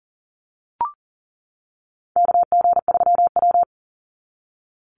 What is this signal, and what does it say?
Morse code, heard as a keyed tone, about 40 wpm, 700 Hz. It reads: KG3W
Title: NA SOTA Chaser Callsigns 40wpm